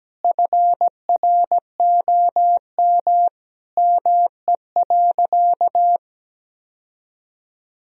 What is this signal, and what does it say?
Morse code, heard as a keyed tone, about 17 wpm, 700 Hz. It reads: FROM ME.
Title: Sentences from Top 1000 Words - Review Part 1 - 17wpm